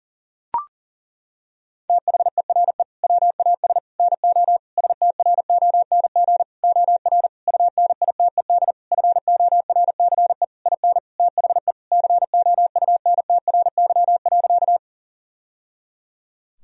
Morse code, heard as a keyed tone, about 40 wpm, 700 Hz. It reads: THERE WAS NO STRONG OR UNITED FORCE IN THE COUNTRY.
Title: Sentences from Top 1000 Words - Review Part 4 - 40wpm